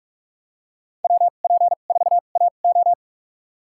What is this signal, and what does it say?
Morse code, heard as a keyed tone, about 46 wpm, spaced as for 32 wpm, 700 Hz. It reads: WP4AO
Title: USA Call Signs - 5 Characters - 2x Character Spacing - 45wpm